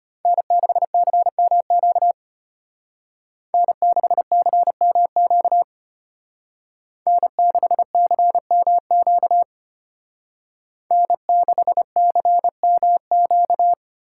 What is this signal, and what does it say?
Morse code, heard as a keyed tone, about 32 wpm, 700 Hz. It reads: N6CMQ N6CMQ N6CMQ N6CMQ